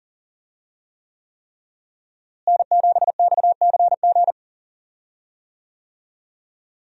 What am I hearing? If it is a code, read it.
Morse code, heard as a keyed tone, about 39 wpm, 700 Hz. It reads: N7XCG
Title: USA Call Signs - 5 Characters - Repeated 3x - 40wpm